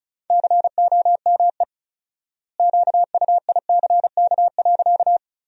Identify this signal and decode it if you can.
Morse code, heard as a keyed tone, about 36 wpm, 700 Hz. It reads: COME QUICK.
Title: 4x Word-Spacing - Sentences from Top 500 Words - 35wpm